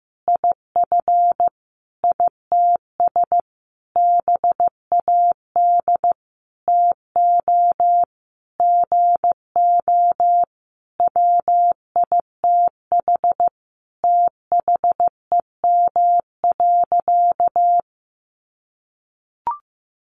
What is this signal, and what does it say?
Morse code, heard as a keyed tone, about 15 wpm, 700 Hz. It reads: IF ITS BAD TO GO WITH THEM.
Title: Sentences from Top 700 Words 15wpm